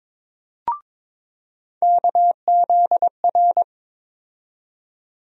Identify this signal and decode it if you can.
Morse code, heard as a keyed tone, about 22 wpm, 700 Hz. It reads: KZR
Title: USA Call Sign Postfixes - 3 Letters 22wpm